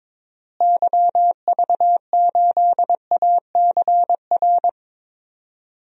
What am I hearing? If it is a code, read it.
Morse code, heard as a keyed tone, about 22 wpm, 700 Hz. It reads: YV8ACR
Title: DXCC-100 Call Signs - 6 Characters - Repeated 3x - 22wpm